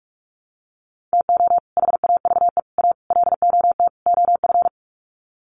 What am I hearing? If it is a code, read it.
Morse code, heard as a keyed tone, about 44 wpm, 700 Hz. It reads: TO HAVE A LOT OF
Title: Sets of 5 Words Encore 45wpm